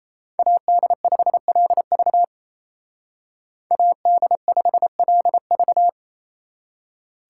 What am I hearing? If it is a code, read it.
Morse code, heard as a keyed tone, about 35 wpm, 700 Hz. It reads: AD5LV AD5LV
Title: Speed Racing - USA Call Signs 28wpm